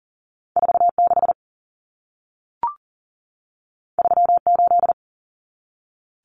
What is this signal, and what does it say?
Morse code, heard as a keyed tone, about 40 wpm, 700 Hz. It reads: 46 38
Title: Numbers - 2-Digit 40wpm